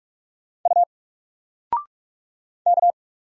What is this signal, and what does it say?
Morse code, heard as a keyed tone, about 48 wpm, 700 Hz. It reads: U K
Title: Single Character (Letter:Number:Punctuation:Prosign) Alt 45wpm